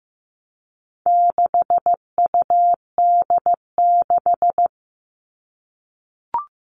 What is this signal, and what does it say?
Morse code, heard as a keyed tone, about 15 wpm, 700 Hz. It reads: 6UD6
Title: All Cut Numbers - 4-Digit 15wpm